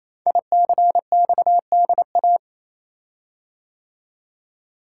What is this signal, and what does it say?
Morse code, heard as a keyed tone, about 28 wpm, 700 Hz. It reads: ICXDA
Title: Code Groups - 5 Letter:Numbers - 28wpm